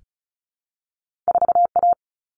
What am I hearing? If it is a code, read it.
Morse code, heard as a keyed tone, about 36 wpm, 700 Hz. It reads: 4A
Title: All Cut Numbers - 2-Digit 35wpm